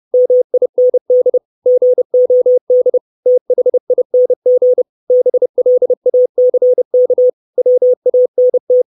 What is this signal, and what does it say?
Morse code, heard as a keyed tone, about 30 wpm, 500 Hz. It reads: MIND GOD THING BLACK WANT
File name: 632 Words 30WPM